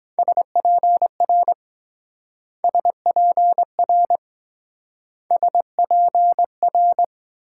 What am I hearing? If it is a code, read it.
Morse code, heard as a keyed tone, about 23 wpm, 700 Hz. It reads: SPR SPR SPR